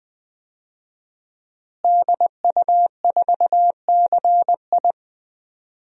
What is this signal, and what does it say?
Morse code, heard as a keyed tone, about 20 wpm, 700 Hz. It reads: DU4CI
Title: DXCC-100 Call Signs - 5 Characters - Repeated 2x - 20wpm